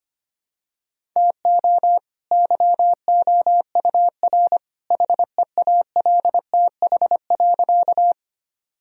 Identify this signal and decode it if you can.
Morse code, heard as a keyed tone, about 25 wpm, 700 Hz. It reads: TO YOUR HEALTH.
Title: Sentences from Top 1000 Words - Review Part 3 - 25wpm